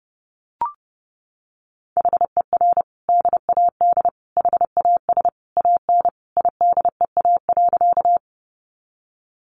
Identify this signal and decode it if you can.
Morse code, heard as a keyed tone, about 30 wpm, 700 Hz. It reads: HER DAD HAS AN IDEA.
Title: Easy Phrases 30wpm